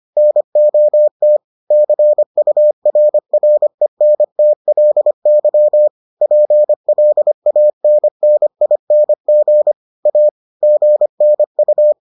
Morse code, heard as a keyed tone, about 25 wpm, 600 Hz. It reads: NOT CURRENTLY PLANNING A GNU